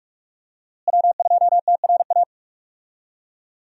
Morse code, heard as a keyed tone, about 46 wpm, 700 Hz. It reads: W2TRA
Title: USA Call Signs - 5 Characters - Repeated 3x - 45wpm